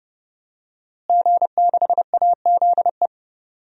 Morse code, heard as a keyed tone, about 30 wpm, 700 Hz. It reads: G6AZE
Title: DXCC-100 Call Signs - 5 Characters - Repeated 2x - 30wpm